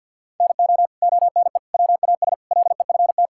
Morse code, heard as a keyed tone, about 50 wpm, 700 Hz. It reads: NO ONE WAS LEFT
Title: Sentences from Top 1000 Words - Review Part 5 - 50wpm